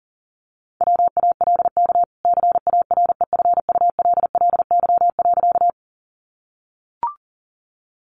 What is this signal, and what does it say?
Morse code, heard as a keyed tone, about 38 wpm, 700 Hz. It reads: WALK CAREFULLY.
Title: Sentences from Top 1300 Words 40wpm